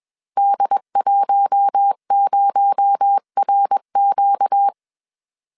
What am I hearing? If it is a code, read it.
Morse code, heard as a keyed tone, about 21 wpm, 800 Hz. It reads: D10RQ
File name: CW / DET / 1 6 1500C S B 20wpm 2048